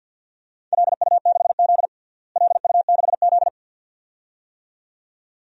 Morse code, heard as a keyed tone, about 48 wpm, 700 Hz. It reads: LU6Z LU6Z
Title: DXCC-100 Call Signs - 4 Characters - Repeated 2x - 50wpm